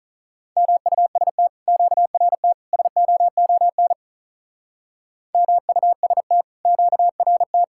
Morse code, heard as a keyed tone, about 43 wpm, 700 Hz. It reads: MUST QRT SOON MUST QRT
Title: Speed-Racing Common QSO Phrases - 35wpm